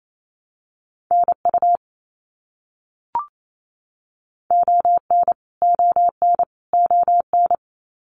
Morse code, heard as a keyed tone, about 28 wpm, 700 Hz. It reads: NU ON ON ON
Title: Canadian Provinces and Territories - Repeated 3x - 28wpm